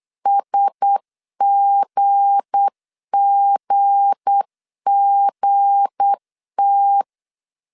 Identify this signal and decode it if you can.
Morse code, heard as a keyed tone, about 8 wpm, 800 Hz. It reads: SGGGT
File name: LEZIONI / 7 GF9Z / 7 G 8wpm 3118